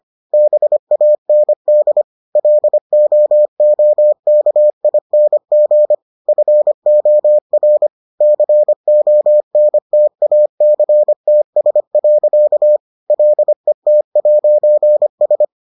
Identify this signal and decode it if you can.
Morse code, heard as a keyed tone, about 25 wpm, 600 Hz. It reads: BAND LOOKING FOR CONTACTS. LET'S